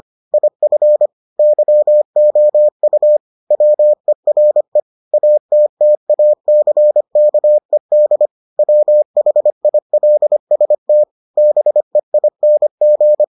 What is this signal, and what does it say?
Morse code, heard as a keyed tone, about 25 wpm, 600 Hz. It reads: IF YOU WERE ATTACKED WHILST BEING